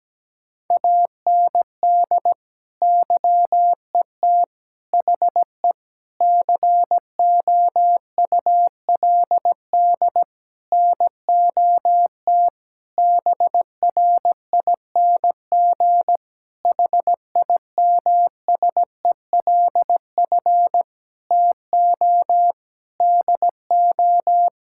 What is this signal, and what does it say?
Morse code, heard as a keyed tone, about 17 wpm, 700 Hz. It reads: AND YET HE COULD NOT BRING HIMSELF TO DO
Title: Sentences from Top 1600 Words 17wpm